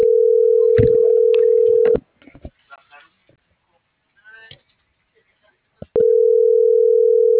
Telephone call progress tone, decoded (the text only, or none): ringback tone